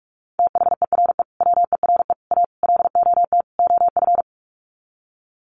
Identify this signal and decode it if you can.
Morse code, heard as a keyed tone, about 44 wpm, 700 Hz. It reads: THERE WERE A LOT OF
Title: Sets of 5 Words 45wpm